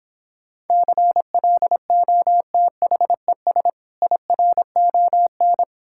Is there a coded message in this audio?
Yes: Morse code from 0.7 to 5.6 s, heard as a keyed tone, about 26 wpm, 700 Hz. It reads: CLOTHES IRON